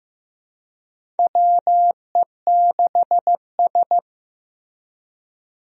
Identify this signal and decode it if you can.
Morse code, heard as a keyed tone, about 15 wpm, 700 Hz. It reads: WE6S
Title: USA Call Signs - 4 Characters - Repeated 3x - 15wpm